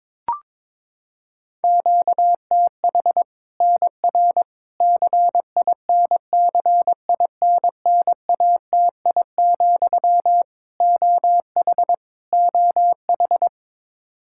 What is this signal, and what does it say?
Morse code, heard as a keyed tone, about 22 wpm, 700 Hz. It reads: QTH NR CINCINNATI, OH OH
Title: Practice QSO - Exchange 1 - Beginning - 22wpm